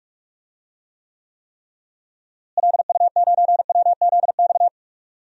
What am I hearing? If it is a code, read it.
Morse code, heard as a keyed tone, about 42 wpm, 700 Hz. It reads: LU9WZX
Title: DXCC-100 Call Signs - 6 Characters - Repeated 3x - 45wpm